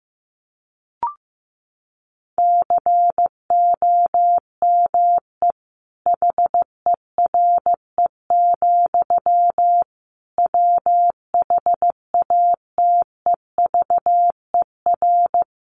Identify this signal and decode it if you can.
Morse code, heard as a keyed tone, about 15 wpm, 700 Hz. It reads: COME HERE, WHATEVER
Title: Sentences from Top 1100 Words 15wpm